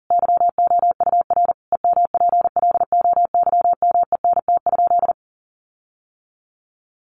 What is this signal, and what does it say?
Morse code, heard as a keyed tone, about 40 wpm, 700 Hz. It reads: YOUR EMPLOYMENT?
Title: Sentences from Top 1500 Words 40wpm